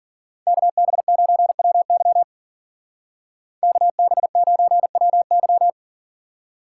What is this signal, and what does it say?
Morse code, heard as a keyed tone, about 43 wpm, 700 Hz. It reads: KB9WY KB9WY
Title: Speed Racing - USA 2x2 Call Signs 40wpm